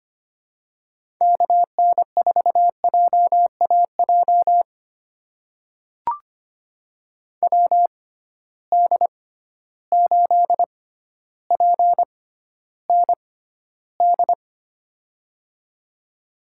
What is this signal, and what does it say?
Morse code, heard as a keyed tone, about 25 wpm, spaced as for 13 wpm, 700 Hz. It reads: KN4JAJ WD8PND
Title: USA Call Signs - 6 Characters - 6x Character Spacing - 25wpm